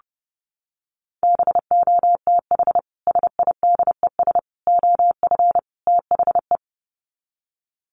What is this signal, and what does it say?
Morse code, heard as a keyed tone, about 30 wpm, 700 Hz. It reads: BOTH SIDES OF THE
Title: Sets of 4 Words 30wpm